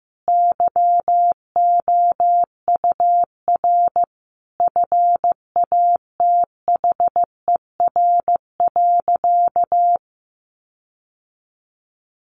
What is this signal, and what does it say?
Morse code, heard as a keyed tone, about 15 wpm, 700 Hz. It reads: YOUR FATHER.
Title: Sentences from Top 1400 Words 15wpm